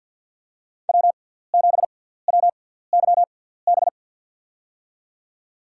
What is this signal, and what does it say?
Morse code, heard as a keyed tone, about 52 wpm, spaced as for 17 wpm, 700 Hz. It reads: W7WYB
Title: USA Call Signs - 5 Characters - 6x Character Spacing - 50wpm